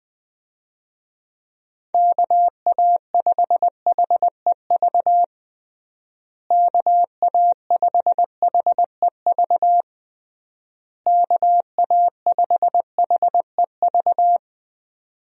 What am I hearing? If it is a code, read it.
Morse code, heard as a keyed tone, about 20 wpm, 700 Hz. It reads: KA5HEV KA5HEV KA5HEV
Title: USA Call Signs - 6 Characters - Repeated 3x - 20wpm